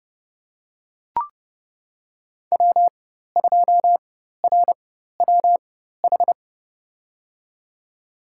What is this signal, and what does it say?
Morse code, heard as a keyed tone, about 30 wpm, spaced as for 14 wpm, 700 Hz. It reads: W2RWH
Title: USA Call Signs - 5 Characters - 4x Character Spacing - 30wpm